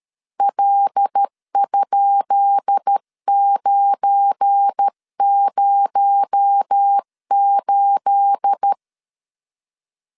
Morse code, heard as a keyed tone, about 13 wpm, 800 Hz. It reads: L?908
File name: LEZIONI / 10 SIGNS / 10B INT 12wpm 2414